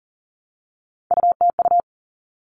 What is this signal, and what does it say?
Morse code, heard as a keyed tone, about 40 wpm, 700 Hz. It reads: UTU
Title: All Cut Numbers - 3-Digit 40wpm